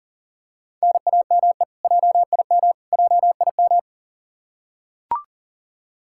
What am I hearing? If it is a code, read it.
Morse code, heard as a keyed tone, about 40 wpm, 700 Hz. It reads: NAME JIM JIM
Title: Practice QSO - Exchange 1 - Beginning - 40wpm